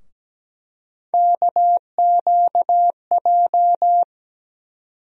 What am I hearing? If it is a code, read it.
Morse code, heard as a keyed tone, about 17 wpm, 700 Hz. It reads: KQJ
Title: USA Call Sign Postfixes - 3 Letters 17wpm